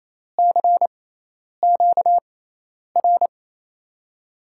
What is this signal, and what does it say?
Morse code, heard as a keyed tone, about 28 wpm, spaced as for 10 wpm, 700 Hz. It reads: CQR
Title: USA Call Signs - 6 Characters - 6x Character Spacing - 28wpm